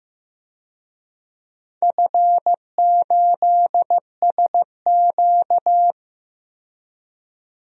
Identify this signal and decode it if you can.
Morse code, heard as a keyed tone, about 15 wpm, 700 Hz. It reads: F8SQ
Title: DXCC-100 Call Signs - 4 Characters - 15wpm